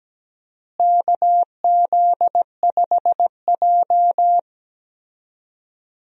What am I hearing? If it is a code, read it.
Morse code, heard as a keyed tone, about 17 wpm, 700 Hz. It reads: KZ5J
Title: USA Call Signs - 4 Characters - Repeated 2x - 17wpm